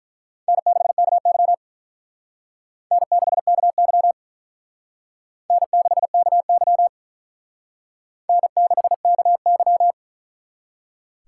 Morse code, heard as a keyed tone, about 44 wpm, 700 Hz. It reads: N6KY N6KY N6KY N6KY